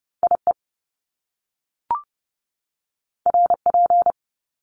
Morse code, heard as a keyed tone, about 30 wpm, 700 Hz. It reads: IE RP
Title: Top Two Letter Combinations in Words 30wpm